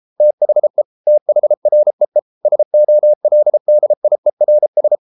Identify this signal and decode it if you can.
Morse code, heard as a keyed tone, about 33 wpm, 600 Hz. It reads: THE THREE SOLDIERS